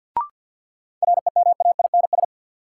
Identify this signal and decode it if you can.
Morse code, heard as a keyed tone, about 47 wpm, 700 Hz. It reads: REMAINS
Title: Sets of 5 Words Encore - New Word Drill - 50wpm